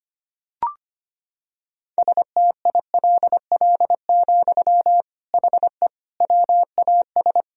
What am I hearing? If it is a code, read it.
Morse code, heard as a keyed tone, about 25 wpm, 700 Hz. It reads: STILL, HE WAS
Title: Sentences from Top 1000 Words - Review Part 1 - 25wpm